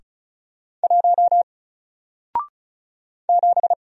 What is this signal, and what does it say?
Morse code, heard as a keyed tone, about 36 wpm, 700 Hz. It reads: 1 7
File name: Single-Letter-Number-35wpm